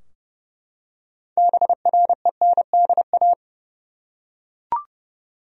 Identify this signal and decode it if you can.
Morse code, heard as a keyed tone, about 30 wpm, 700 Hz. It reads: BRENDA
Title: USA Common Names - 30wpm